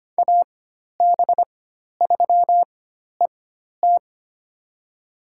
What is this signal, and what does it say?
Morse code, heard as a keyed tone, about 25 wpm, spaced as for 12 wpm, 700 Hz. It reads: AB3ET
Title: USA Call Signs - 4x Character Spacing - 25wpm